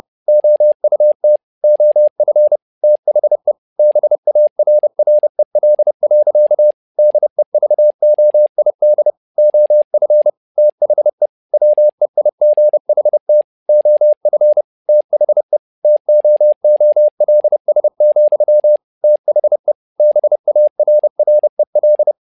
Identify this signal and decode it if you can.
Morse code, heard as a keyed tone, about 30 wpm, 600 Hz. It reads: OUT OF THE BARREL. DEVOID OF THE WEIGHT OF THE TOOLS, THE BARREL
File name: learning center / tower trouble / tower trouble 30wpm